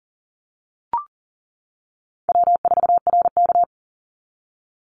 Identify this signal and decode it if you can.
Morse code, heard as a keyed tone, about 40 wpm, 700 Hz. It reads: W4RK
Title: NA SOTA Chaser Callsigns 40wpm